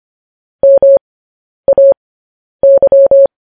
Morse code, heard as a keyed tone, about 25 wpm, spaced as for 10 wpm, 550 Hz. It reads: MAY
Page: news headlines in morse code at 10 wpm